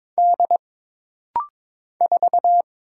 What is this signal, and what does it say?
Morse code, heard as a keyed tone, about 22 wpm, 700 Hz. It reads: D 4
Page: This rapid-fire timing is geared to push your ICR (Instant Character Recognition) skills.